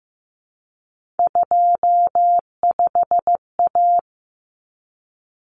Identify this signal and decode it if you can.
Morse code, heard as a keyed tone, about 15 wpm, 700 Hz. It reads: 25A
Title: Common Cut Numbers - 3-Digit 15wpm